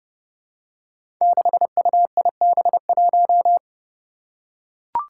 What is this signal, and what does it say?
Morse code, heard as a keyed tone, about 30 wpm, 700 Hz. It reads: 6UIB1
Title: Five Challenging Letter-Number Combinations 30wpm